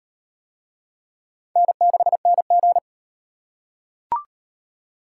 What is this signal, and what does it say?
Morse code, heard as a keyed tone, about 35 wpm, 700 Hz. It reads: N6NG